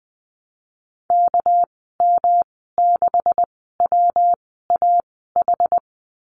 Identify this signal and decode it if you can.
Morse code, heard as a keyed tone, about 20 wpm, spaced as for 14 wpm, 700 Hz. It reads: KM6WAH
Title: 213 Lesson 34g - Number 8 - Call Signs 20wpm